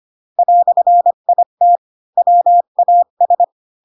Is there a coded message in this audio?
Yes: Morse code from 0.4 to 3.4 s, heard as a keyed tone, about 25 wpm, 700 Hz. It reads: "IT WAS